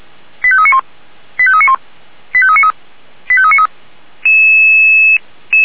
Five-tone selective call (ZVEI), heard as quick runs of five tones, tones of about 70 ms. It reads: 85391 85391 85393 85393